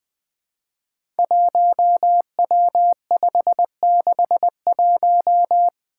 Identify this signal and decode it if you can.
Morse code, heard as a keyed tone, about 21 wpm, 700 Hz. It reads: 1W561
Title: Five Challenging Letter-Number Combinations 20wpm